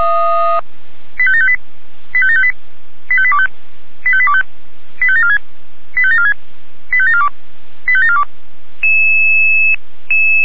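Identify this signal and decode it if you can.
Five-tone selective call (ZVEI), heard as quick runs of five tones, tones of about 70 ms. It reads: 85758 85758 85825 85825 85746 85746 85742 85742
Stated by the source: ZVEI five-tone call received by SDR.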